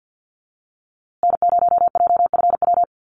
Morse code, heard as a keyed tone, about 50 wpm, 700 Hz. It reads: N0JFW
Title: USA Call Signs Repeated 2x 50wpm